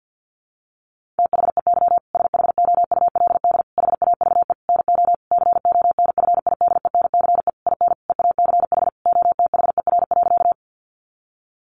Morse code, heard as a keyed tone, about 50 wpm, 700 Hz. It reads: THEY SHOULD HAVE NO CONFIDENCE IN EACH OTHER.
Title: Sentences from Top 1500 Words 50wpm